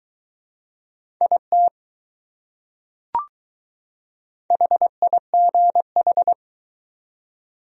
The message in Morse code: IT HIGH

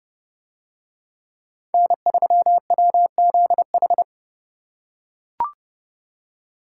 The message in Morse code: N3WZH

